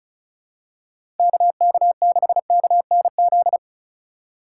Morse code, heard as a keyed tone, about 34 wpm, 700 Hz. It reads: KK6KNZ
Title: USA Call Signs - 4x Character Spacing - 35wpm